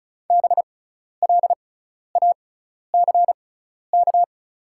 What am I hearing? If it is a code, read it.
Morse code, heard as a keyed tone, about 36 wpm, spaced as for 12 wpm, 700 Hz. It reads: BLACK
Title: 6x Character Spacing - Top 500 Words - 35wpm